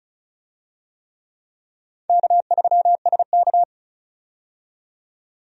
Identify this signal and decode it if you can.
Morse code, heard as a keyed tone, about 36 wpm, 700 Hz. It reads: K3SK
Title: USA Call Signs - 4 Characters - 35wpm